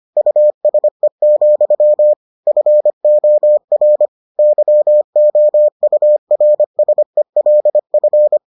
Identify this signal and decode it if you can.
Morse code, heard as a keyed tone, about 25 wpm, 600 Hz. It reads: USE, FOR YOURSELF